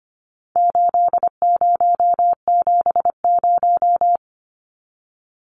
Morse code, heard as a keyed tone, about 25 wpm, 700 Hz. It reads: 8070
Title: Numbers - 4-Digit 25wpm